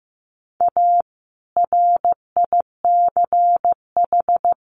Morse code, heard as a keyed tone, about 15 wpm, 700 Hz. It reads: A RICH
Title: Sentences from Top 1100 Words 15wpm